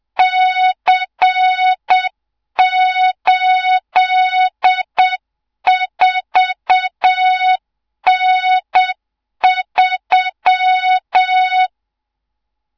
Morse code, heard as a keyed tone, about 7 wpm, 3700 Hz, with noise. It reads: C84N3
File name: Morze-HA2MN / hang / 24 GYAKORLAT